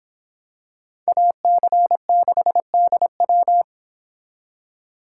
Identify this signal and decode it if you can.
Morse code, heard as a keyed tone, about 26 wpm, 700 Hz. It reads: AC6DW